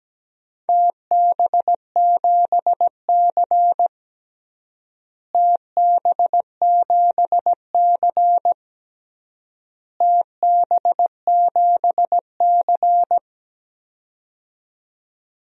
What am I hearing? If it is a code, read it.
Morse code, heard as a keyed tone, about 17 wpm, 700 Hz. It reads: TB7C TB7C TB7C
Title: DXCC-100 Call Signs - 4 Characters - Repeated 3x - 17wpm